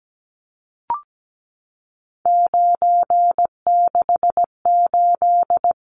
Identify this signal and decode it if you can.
Morse code, heard as a keyed tone, about 17 wpm, 700 Hz. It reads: 968
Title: Numbers - 3-Digit 17wpm